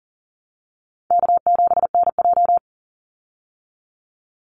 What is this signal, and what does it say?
Morse code, heard as a keyed tone, about 38 wpm, 700 Hz. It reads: K7NJ
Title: Easy Call Signs Repeated 2x - 40wpm